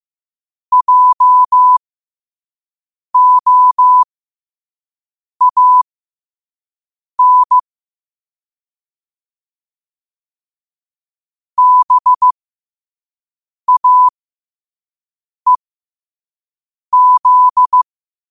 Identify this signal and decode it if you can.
Morse code, heard as a keyed tone, about 15 wpm, spaced as for 5 wpm, 1000 Hz. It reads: JOAN BAEZ
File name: qotdfiles / 5wpm